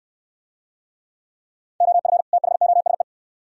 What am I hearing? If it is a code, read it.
Morse code, heard as a keyed tone, about 65 wpm, 700 Hz. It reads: OF THOSE